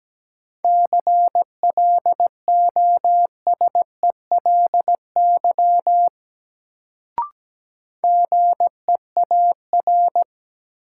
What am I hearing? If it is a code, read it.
Morse code, heard as a keyed tone, about 17 wpm, 700 Hz. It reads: CLOSELY GEAR
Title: Sentences from Top 1400 Words - New Word Drill - 17wpm